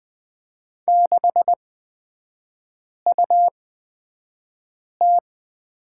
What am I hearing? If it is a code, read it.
Morse code, heard as a keyed tone, about 20 wpm, spaced as for 5 wpm, 690 Hz. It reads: 6UT
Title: 229 Lesson 37e - Letter x - Three Characters 20wpm